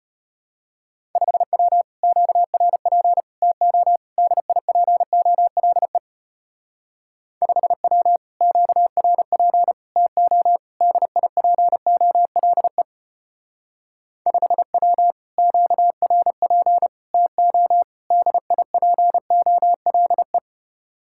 Morse code, heard as a keyed tone, about 32 wpm, 700 Hz. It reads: 5W QRP TO DIPOLE 5W QRP TO DIPOLE 5W QRP TO DIPOLE